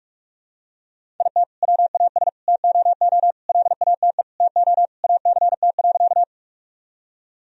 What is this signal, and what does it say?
Morse code, heard as a keyed tone, about 46 wpm, 700 Hz. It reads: IT WAS TOO LATE TO ACT.
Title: Sentences from Top 1000 Words - Review Part 1 - 45wpm